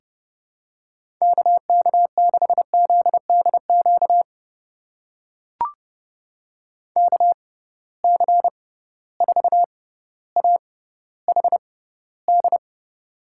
Morse code, heard as a keyed tone, about 30 wpm, spaced as for 15 wpm, 700 Hz. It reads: KK6ZDQ KC4AHD
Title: USA Call Signs - 6 Characters - 6x Character Spacing - 30wpm